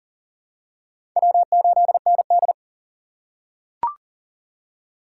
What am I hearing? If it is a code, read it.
Morse code, heard as a keyed tone, about 40 wpm, 700 Hz. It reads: W8ND
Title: USA Call Signs - 4 Characters - 3x Character Spacing - 40wpm